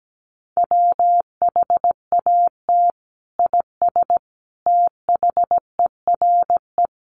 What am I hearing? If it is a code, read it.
Morse code, heard as a keyed tone, about 17 wpm, 700 Hz. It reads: WHAT IS THERE
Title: Sentences from Top 100 Words 17wpm